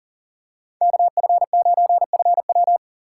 Morse code, heard as a keyed tone, about 40 wpm, 700 Hz. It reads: KF9FW
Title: USA Call Signs - 5 Characters - 4x Character Spacing - 40wpm